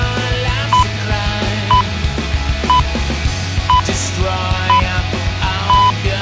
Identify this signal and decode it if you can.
Radio time signal (time pips). It is heard from 0.7 to 5.9 s.